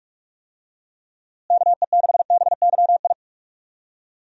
Morse code, heard as a keyed tone, about 41 wpm, 700 Hz. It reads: KE6BYI
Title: USA Call Signs - 6 Characters - Repeated 3x - 45wpm